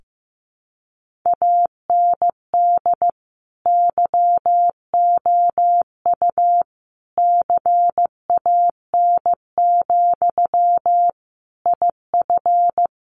Morse code, heard as a keyed tone, about 15 wpm, 700 Hz. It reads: AND YOU CAN, IF
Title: Sentences from Top 1500 Words 15wpm